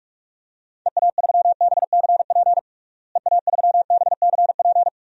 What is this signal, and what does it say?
Morse code, heard as a keyed tone, about 46 wpm, 700 Hz. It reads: EA3BCP EA3BCP
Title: DXCC-100 Call Signs - 6 Characters - Repeated 3x - 45wpm